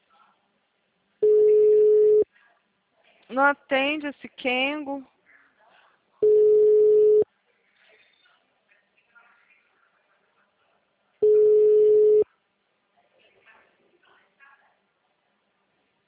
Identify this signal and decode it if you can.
Ringback tone (Europe, 425 Hz), with speech.